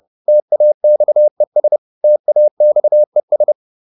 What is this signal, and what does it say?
Morse code, heard as a keyed tone, about 30 wpm, 600 Hz. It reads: TAXES TAXES